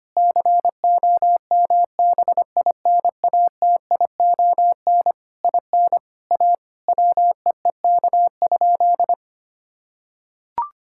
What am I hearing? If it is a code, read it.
Morse code, heard as a keyed tone, about 25 wpm, 700 Hz. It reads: COMBINATION IN A WEEK?
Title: Sentences from Top 1000 Words - Review Part 6 - 25wpm